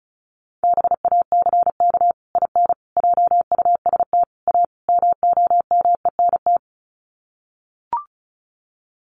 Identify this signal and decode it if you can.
Morse code, heard as a keyed tone, about 34 wpm, 700 Hz. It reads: BACK IN JUST A MOMENT
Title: Sets of 5 Words Encore 35wpm